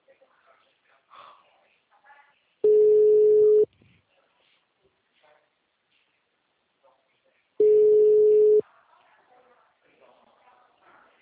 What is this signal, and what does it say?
Ringback tone (Europe, 425 Hz), with speech.
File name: out- -2017- - - .85194